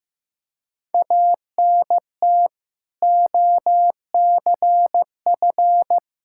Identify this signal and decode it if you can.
Morse code, heard as a keyed tone, about 15 wpm, 700 Hz. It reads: ANT OCF
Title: K1USN SST QSO Practice 15wpm